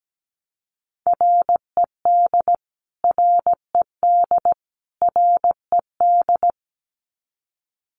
Morse code, heard as a keyed tone, about 17 wpm, 700 Hz. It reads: RED RED RED
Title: Top 500 Words Repeated 17wpm